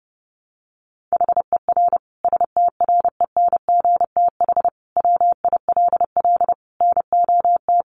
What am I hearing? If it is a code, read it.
Morse code, heard as a keyed tone, about 30 wpm, 700 Hz. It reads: HER STRENGTH WILL NOT
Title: Sentences from Top 1300 Words 30wpm